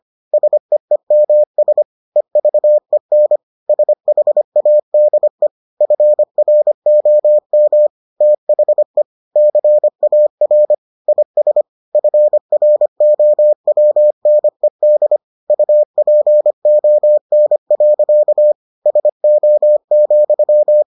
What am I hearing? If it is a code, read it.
Morse code, heard as a keyed tone, about 25 wpm, 600 Hz. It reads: SEEMS EVEN SHADE FROM THE CAR IS FROWNED UPON. SO,